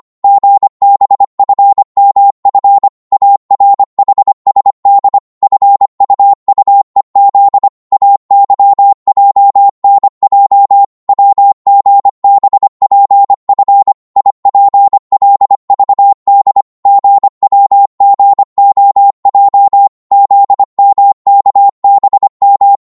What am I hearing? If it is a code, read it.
Morse code, heard as a keyed tone, about 25 wpm, 800 Hz. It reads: GBFMF ARHSD FUUEZ AYJNJ WGBPF IPLVD GWGOJ ZMKBM
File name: Morse / ltr-25wpm